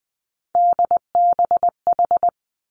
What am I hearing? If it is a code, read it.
Morse code, heard as a keyed tone, about 20 wpm, 700 Hz. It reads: DBH
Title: Three Challenging Letter-Number Combinations 20wpm